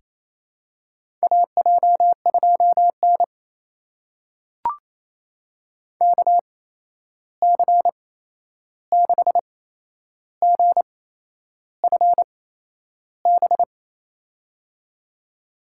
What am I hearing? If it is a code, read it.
Morse code, heard as a keyed tone, about 28 wpm, spaced as for 11 wpm, 700 Hz. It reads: AJ2N KC6GFB
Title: USA Call Signs - 8x Character Spacing - 28wpm